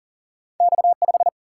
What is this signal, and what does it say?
Morse code, heard as a keyed tone, about 40 wpm, 700 Hz. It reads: X5
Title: Single-Letter Single-Number 40wpm